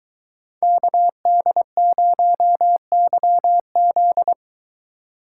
Morse code, heard as a keyed tone, about 22 wpm, 700 Hz. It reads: KD0YZ